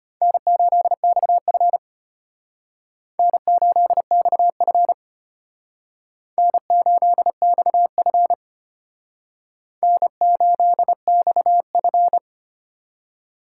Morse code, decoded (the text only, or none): N8XF N8XF N8XF N8XF